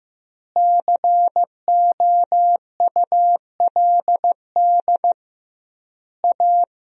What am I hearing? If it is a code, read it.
Morse code, heard as a keyed tone, about 15 wpm, 700 Hz. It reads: COULD A
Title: 2x Word-Spacing - Sentences from Top 200 Words - 15wpm